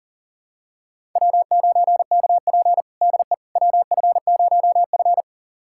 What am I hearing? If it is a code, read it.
Morse code, heard as a keyed tone, about 40 wpm, 700 Hz. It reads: W9KP DE WF0F
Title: USA Callsign DE Callsign - 4 Characters - 40wpm